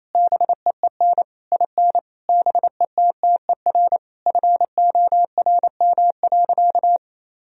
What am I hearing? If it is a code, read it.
Morse code, heard as a keyed tone, about 28 wpm, 700 Hz. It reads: BEEN IN BETTER FORM.
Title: Sentences from Top 1000 Words - Review Part 1 - 28wpm